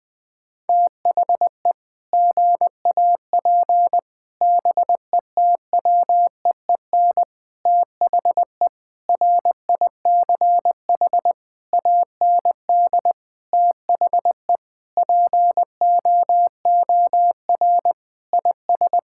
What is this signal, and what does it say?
Morse code, heard as a keyed tone, about 20 wpm, 700 Hz. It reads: THE GAP BETWEEN THE RICH AND THE POOR IS
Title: Sentences with 8-Word Phrases - Spoken Beginning - 20wpm